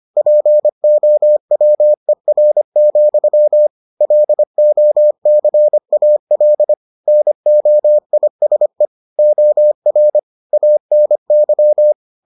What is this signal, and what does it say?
Morse code, heard as a keyed tone, about 25 wpm, 600 Hz. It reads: POWER, LOCAL NOISE OR ANY